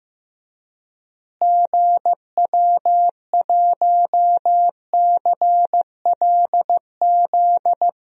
Morse code, heard as a keyed tone, about 15 wpm, 700 Hz. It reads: GW1CLZ
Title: DXCC-100 Call Signs - 6 Characters - Repeated 3x - 15wpm